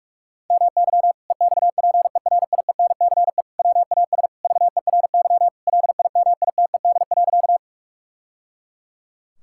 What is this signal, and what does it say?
Morse code, heard as a keyed tone, about 41 wpm, 700 Hz. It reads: MY EXPERIENCE WAS VERY LIMITED.
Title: Sentences from Top 1000 Words - Review Part 6 - 45wpm